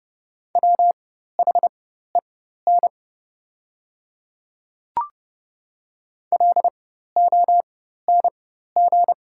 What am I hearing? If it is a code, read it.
Morse code, heard as a keyed tone, about 30 wpm, spaced as for 14 wpm, 700 Hz. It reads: WHEN LONG
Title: 4x Character Spacing - Top 100 Words - 30wpm